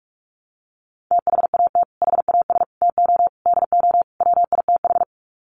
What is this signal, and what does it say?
Morse code, heard as a keyed tone, about 44 wpm, 700 Hz. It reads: THAT HAS TO DO WITH
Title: Sets of 5 Words Encore2 45wpm